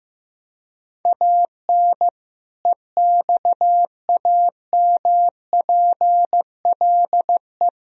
Morse code, heard as a keyed tone, about 15 wpm, 700 Hz. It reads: AN EXAMPLE
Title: Sentences from Top 1000 Words - Review Part 1 - 15wpm